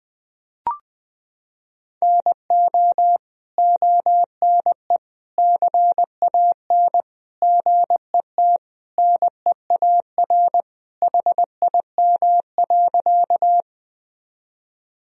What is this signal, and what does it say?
Morse code, heard as a keyed tone, about 21 wpm, 700 Hz. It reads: NO ONE CAN GET NEAR HIM.
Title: Sentences from Top 1000 Words - Review Part 1 - 20wpm